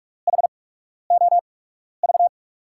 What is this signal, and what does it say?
Morse code, heard as a keyed tone, about 45 wpm, spaced as for 12 wpm, 700 Hz. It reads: HOV
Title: USA Call Signs - 5 Characters - 8x Character Spacing - 45wpm